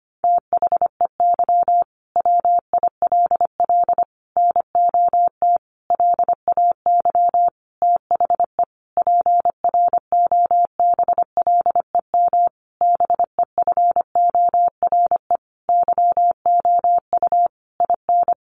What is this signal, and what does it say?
Morse code, heard as a keyed tone, about 25 wpm, 700 Hz. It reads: THEY WILL NOT LAY THE PROBLEM BEFORE YOU IN
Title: Sentences from Top 1500 Words 25wpm